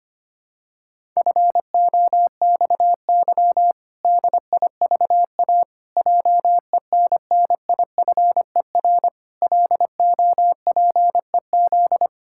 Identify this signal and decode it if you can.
Morse code, heard as a keyed tone, about 25 wpm, 700 Hz. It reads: FOXY DIVA JENNIFER LOPEZ
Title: Pangram Challenge v2 - 25wpm